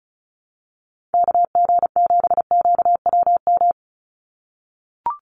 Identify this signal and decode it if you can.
Morse code, heard as a keyed tone, about 34 wpm, 700 Hz. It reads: KG7QWM
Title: USA Call Signs 35wpm